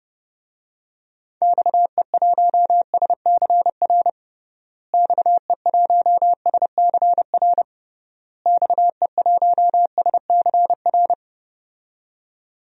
Morse code, heard as a keyed tone, about 30 wpm, 700 Hz. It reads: XE1SCR XE1SCR XE1SCR
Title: DXCC-100 Call Signs - 6 Characters - Repeated 3x - 30wpm